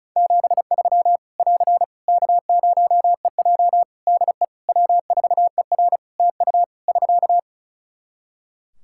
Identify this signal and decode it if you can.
Morse code, heard as a keyed tone, about 36 wpm, 700 Hz. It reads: 73 <AR> K0EJ DE W4ER TU <SK>
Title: Practice QSO - Exchange 4 - Ending - 35wpm